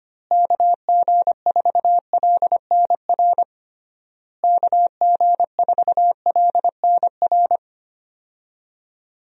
Morse code, heard as a keyed tone, about 25 wpm, 700 Hz. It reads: KG4LNR KG4LNR
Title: USA Call Signs - 6 Characters - Repeated 3x - 25wpm